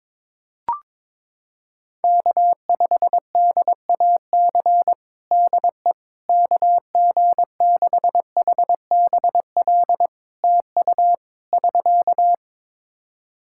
Morse code, heard as a keyed tone, about 22 wpm, 700 Hz. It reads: K5DAC DE KG6HBL TU <SK>
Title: K1USN SST QSO Practice 22wpm